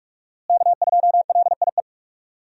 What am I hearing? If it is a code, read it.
Morse code, heard as a keyed tone, about 47 wpm, 700 Hz. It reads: K2LIE
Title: USA Call Signs - 5 Characters - Repeated 2x - 45wpm